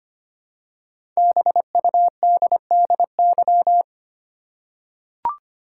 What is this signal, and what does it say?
Morse code, heard as a keyed tone, about 25 wpm, 700 Hz. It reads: BUDDY
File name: 5 Letter Words - First 3 Letters Given - 25wpm